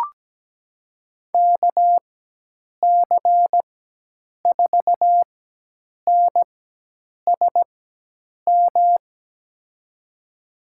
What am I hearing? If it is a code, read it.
Morse code, heard as a keyed tone, about 17 wpm, spaced as for 8 wpm, 700 Hz. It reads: KC4NSM
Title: USA Call Signs - 6 Characters - 4x Character Spacing - 17wpm